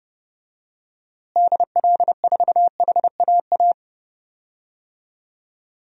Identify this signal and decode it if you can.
Morse code, heard as a keyed tone, about 32 wpm, 700 Hz. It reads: DL4HAA
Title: DXCC-100 Call Signs - 6 Characters - Repeated 3x - 30wpm